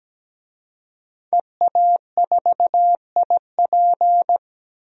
Morse code, heard as a keyed tone, about 17 wpm, 700 Hz. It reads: EA4IP
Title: DXCC-100 Call Signs - 5 Characters - 17wpm